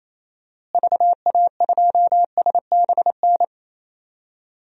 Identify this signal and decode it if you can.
Morse code, heard as a keyed tone, about 27 wpm, 700 Hz. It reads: VA2SBN
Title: DXCC-100 Call Signs - 6 Characters - Repeated 2x - 28wpm